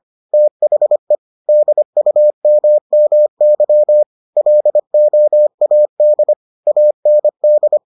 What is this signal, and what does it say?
Morse code, heard as a keyed tone, about 25 wpm, 600 Hz. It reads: THE DUMMY LOAD AND